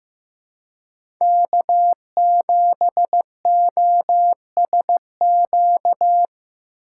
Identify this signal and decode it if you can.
Morse code, heard as a keyed tone, about 15 wpm, 700 Hz. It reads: K7OSQ
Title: USA Call Signs - 5 Characters - 3x Character Spacing - 15wpm